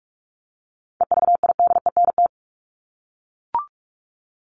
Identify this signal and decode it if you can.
Morse code, heard as a keyed tone, about 44 wpm, 700 Hz. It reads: EVIDENT
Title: LY Suffix - Top 100 Words with Base Words 45wpm